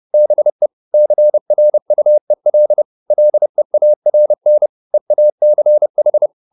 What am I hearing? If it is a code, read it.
Morse code, heard as a keyed tone, about 30 wpm, 600 Hz. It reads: BE CRUEL LEARN EACH